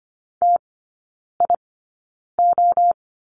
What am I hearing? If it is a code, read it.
Morse code, heard as a keyed tone, about 25 wpm, spaced as for 9 wpm, 700 Hz. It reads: TIO
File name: Lesson 17e - Letter u - Three Characters 25wpm